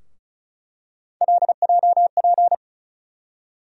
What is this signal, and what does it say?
Morse code, heard as a keyed tone, about 34 wpm, 700 Hz. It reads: LJP
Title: USA Call Sign Postfixes - 3 Letters 35wpm